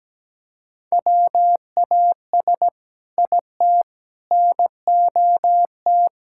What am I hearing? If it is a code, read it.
Morse code, heard as a keyed tone, about 17 wpm, 700 Hz. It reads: WAS IT NOT
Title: Sentences from Top 1600 Words 17wpm